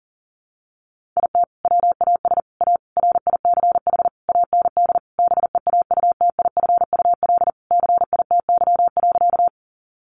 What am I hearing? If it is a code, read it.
Morse code, heard as a keyed tone, about 40 wpm, 700 Hz. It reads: IT WAS A RICH AND BEAUTIFUL CITY.
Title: Sentences from Top 1100 Words 40wpm